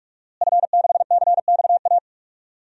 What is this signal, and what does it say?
Morse code, heard as a keyed tone, about 46 wpm, 700 Hz. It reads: F6CXA
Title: DXCC-100 Call Signs - 5 Characters - Repeated 3x - 45wpm